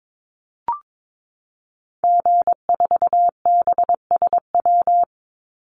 Morse code, heard as a keyed tone, about 22 wpm, 700 Hz. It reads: G4BSW
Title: Call Signs Repeated 2x - 22wpm